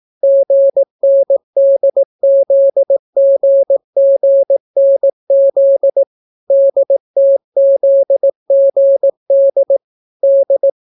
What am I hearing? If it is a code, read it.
Morse code, heard as a keyed tone, about 18 wpm, 550 Hz. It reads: GNDZGGNZ DTZGD D